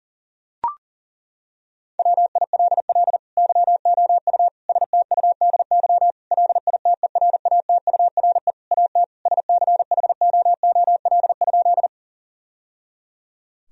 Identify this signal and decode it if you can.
Morse code, heard as a keyed tone, about 40 wpm, 700 Hz. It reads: WILL YOU STUDY LITERATURE AT SCHOOL?
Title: Sentences from Top 1000 Words - Review Part 3 - 40wpm